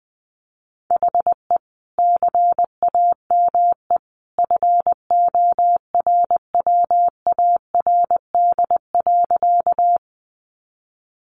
Morse code, heard as a keyed tone, about 20 wpm, 700 Hz. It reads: HE CAME FORWARD.
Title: Sentences from Top 1300 Words 20wpm